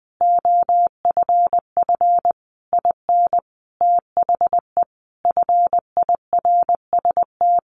Morse code, heard as a keyed tone, about 20 wpm, 700 Hz. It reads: OFF IN THE FIRST
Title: Sentences from Top 200 Words 20wpm